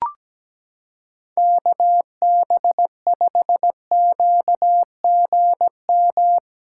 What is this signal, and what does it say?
Morse code, heard as a keyed tone, about 17 wpm, 700 Hz. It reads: KB5QGM
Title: USA Callsign DE Callsign 17wpm